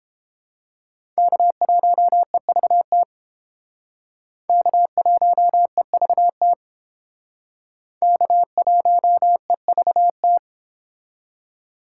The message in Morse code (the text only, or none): K1EVT K1EVT K1EVT